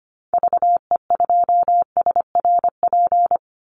Morse code, heard as a keyed tone, about 25 wpm, 700 Hz. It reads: VE2SRP
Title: Call Signs Repeated 3x - 25wpm